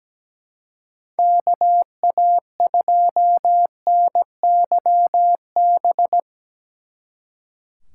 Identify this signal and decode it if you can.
Morse code, heard as a keyed tone, about 18 wpm, 700 Hz. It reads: KA2NYB
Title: Speed Racing - USA 2x3 Call Signs 17wpm